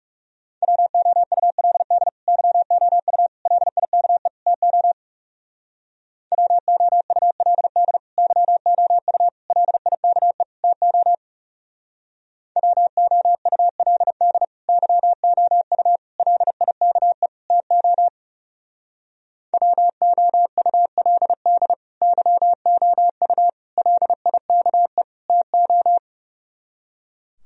Morse code, heard as a keyed tone, about 39 wpm, 690 Hz. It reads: WOULD YOU LIKE TO WOULD YOU LIKE TO WOULD YOU LIKE TO WOULD YOU LIKE TO